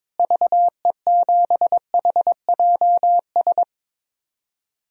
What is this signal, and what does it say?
Morse code, heard as a keyed tone, about 22 wpm, 700 Hz. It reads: VE7HJS
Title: DXCC-100 Call Signs - 6 Characters - Repeated 2x - 22wpm